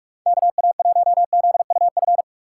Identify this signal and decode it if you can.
Morse code, heard as a keyed tone, about 46 wpm, 700 Hz. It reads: KA1ZUF
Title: USA Call Signs - 6 Characters - 3x Character Spacing - 45wpm